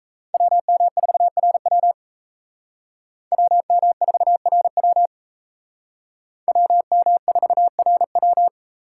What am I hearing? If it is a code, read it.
Morse code, heard as a keyed tone, about 35 wpm, 700 Hz. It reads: WM4RW WM4RW WM4RW